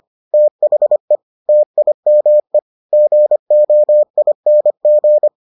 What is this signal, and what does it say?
Morse code, heard as a keyed tone, about 25 wpm, 600 Hz. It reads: THE TIME GOING